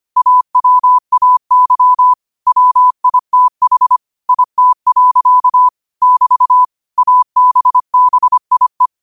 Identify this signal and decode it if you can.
Morse code, heard as a keyed tone, about 25 wpm, 1000 Hz. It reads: AWAY WITH IT. <BT> ABBIE
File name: qotdfiles / 25wpm